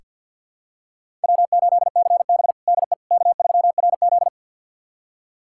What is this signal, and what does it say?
Morse code, heard as a keyed tone, about 49 wpm, 700 Hz. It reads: W8CB DE K3RZ
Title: USA Callsign DE Callsign - 4 Characters - 50wpm